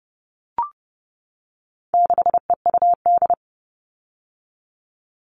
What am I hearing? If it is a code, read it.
Morse code, heard as a keyed tone, about 30 wpm, 700 Hz. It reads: 6EUD
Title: All Cut Numbers - 4-Digit 30wpm